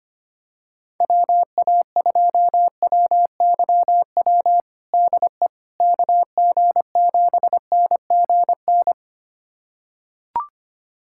Morse code, heard as a keyed tone, about 25 wpm, 700 Hz. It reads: WA2WYW DE KG7NGN
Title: USA Callsign DE Callsign - 6 Characters - 25wpm